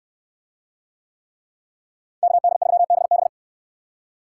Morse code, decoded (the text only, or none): KD3BB